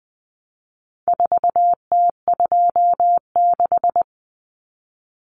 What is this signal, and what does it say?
Morse code, heard as a keyed tone, about 20 wpm, 700 Hz. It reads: 4T26
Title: Common Cut Numbers - 4-Digit 20wpm